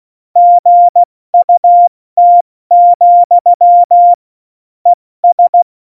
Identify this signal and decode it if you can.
Morse code, heard as a keyed tone, about 16 wpm, spaced as for 14 wpm, 700 Hz. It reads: GUT, ES
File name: Morsekurs / Witze / Witz 13. Oktober 2020, 70 F80